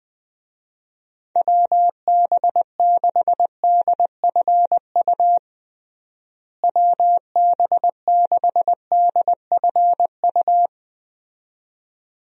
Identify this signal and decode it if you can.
Morse code, heard as a keyed tone, about 20 wpm, 700 Hz. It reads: WB6DFU WB6DFU
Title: USA Call Signs - 6 Characters - Repeated 2x - 20wpm